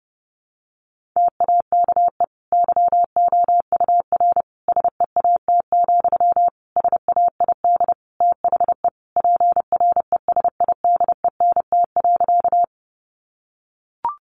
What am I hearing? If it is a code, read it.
Morse code, heard as a keyed tone, about 30 wpm, 700 Hz. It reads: TAKE YOUR SEAT, SAID THE PRESIDENT.
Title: Sentences from Top 1500 Words 30wpm